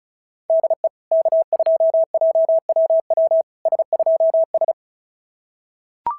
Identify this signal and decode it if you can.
Morse code, heard as a keyed tone, about 33 wpm, 650 Hz. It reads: DE K2JWW S2S
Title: SOTA QSO Practice 35wpm